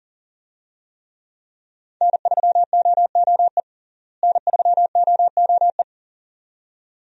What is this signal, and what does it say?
Morse code, heard as a keyed tone, about 40 wpm, 700 Hz. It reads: N3OOE N3OOE
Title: USA Call Signs - 5 Characters - Repeated 2x - 40wpm